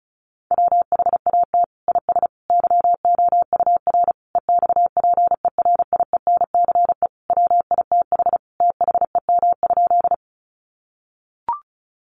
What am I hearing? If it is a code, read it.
Morse code, heard as a keyed tone, about 34 wpm, 700 Hz. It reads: WHAT IS YOUR EXPERIENCE WITH THEM?
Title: Sentences from Top 600 Words 35wpm